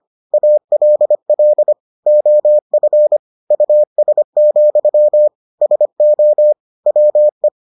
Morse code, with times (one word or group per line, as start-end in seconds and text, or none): 0.3-1.7 ALL
2.1-3.2 OF
3.5-5.3 US,
5.6-6.5 SO
6.9-7.5 WE